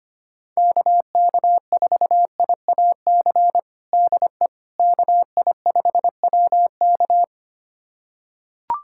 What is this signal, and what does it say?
Morse code, heard as a keyed tone, about 25 wpm, 700 Hz. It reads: KK4IAC DE KI5WK
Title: USA Callsign DE Callsign 25wpm